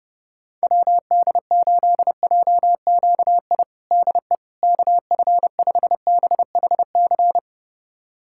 Morse code, heard as a keyed tone, about 32 wpm, 700 Hz. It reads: WD8JQI DE KF5BHC
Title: USA Callsign DE Callsign 30wpm